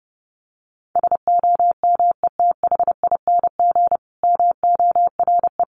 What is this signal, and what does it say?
Morse code, heard as a keyed tone, about 28 wpm, 700 Hz. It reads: SOMETHING MORE
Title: Sentences from Top 1400 Words 30wpm